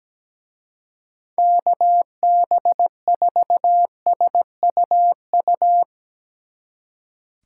Morse code, heard as a keyed tone, about 18 wpm, 700 Hz. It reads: KB4SUU
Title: Speed Racing - USA 2x3 Call Signs 17wpm